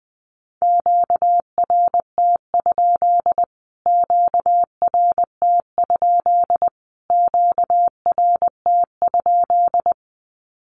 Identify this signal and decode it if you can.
Morse code, heard as a keyed tone, about 20 wpm, 700 Hz. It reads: QRT? QRT? QRT?
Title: Q-Codes Repeated 3x 20wpm